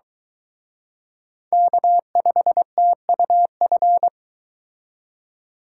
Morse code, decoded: K5TUF

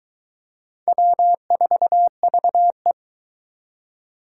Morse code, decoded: W4VE

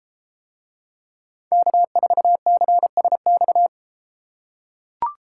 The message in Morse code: K4CSX

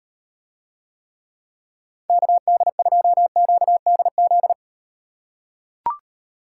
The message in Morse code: KD2QDZ

